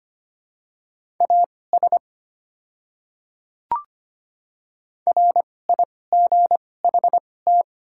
Morse code, heard as a keyed tone, about 25 wpm, spaced as for 18 wpm, 690 Hz. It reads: AS RIGHT
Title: 2x Character Spacing - Top 200 Words - 25wpm